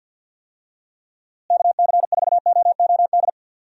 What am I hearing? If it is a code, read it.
Morse code, heard as a keyed tone, about 50 wpm, 700 Hz. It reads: KC4OOD
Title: USA Call Signs - 6 Characters - Repeated 3x - 50wpm